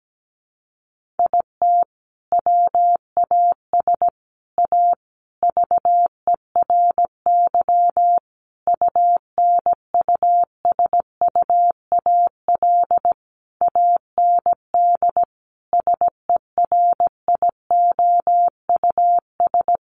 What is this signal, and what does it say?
Morse code, heard as a keyed tone, about 17 wpm, 700 Hz. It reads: IT WAS A VERY UNUSUAL AND SERIOUS
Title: Sentences from Top 1400 Words 17wpm